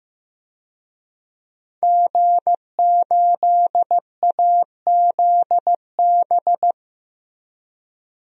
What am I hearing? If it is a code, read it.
Morse code, heard as a keyed tone, about 15 wpm, 700 Hz. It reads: G8AZB
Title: DXCC-100 Call Signs - 5 Characters - 15wpm